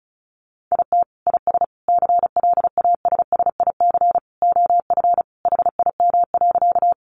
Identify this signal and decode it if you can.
Morse code, heard as a keyed tone, about 34 wpm, 700 Hz. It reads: IT IS CLASSIC OF HIM.
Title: Sentences from Top 1000 Words 35wpm